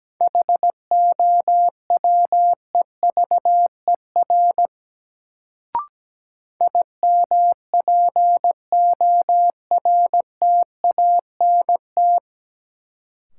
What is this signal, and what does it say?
Morse code, heard as a keyed tone, about 17 wpm, 700 Hz. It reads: HOWEVER IMPORTANT
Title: Sentences from Top 600 Words - New Word Drill - 17wpm